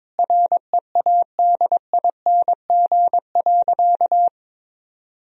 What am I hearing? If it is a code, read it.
Morse code, heard as a keyed tone, about 22 wpm, 700 Hz. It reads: READING.
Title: Sentences from Top 1000 Words - Review Part 3 - 22wpm